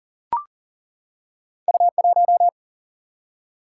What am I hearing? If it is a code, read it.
Morse code, heard as a keyed tone, about 39 wpm, 700 Hz. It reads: U 1
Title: Single-Letter Single-Number 40wpm